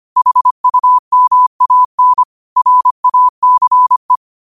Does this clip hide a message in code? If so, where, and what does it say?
Yes: Morse code from 0.2 to 4.2 s, heard as a keyed tone, about 25 wpm, 1000 Hz. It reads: HUMAN RACE